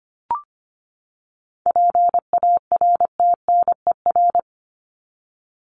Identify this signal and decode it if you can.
Morse code, heard as a keyed tone, about 25 wpm, 700 Hz. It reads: PARTNER
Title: Top 1501-1750 Words 25wpm